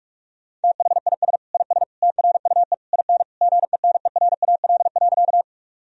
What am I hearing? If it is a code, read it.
Morse code, heard as a keyed tone, about 43 wpm, 700 Hz. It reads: THIS IS TRUE IN GENERAL.
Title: Sentences from Top 1000 Words - Review Part 3 - 45wpm